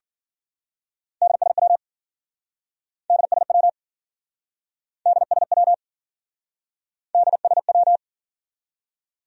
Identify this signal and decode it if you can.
Morse code, heard as a keyed tone, about 54 wpm, 700 Hz. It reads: DSW DSW DSW DSW